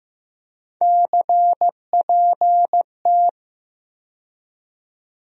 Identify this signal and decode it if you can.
Morse code, heard as a keyed tone, about 15 wpm, 700 Hz. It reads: CPT
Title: USA Call Sign Postfixes - 3 Letters 15wpm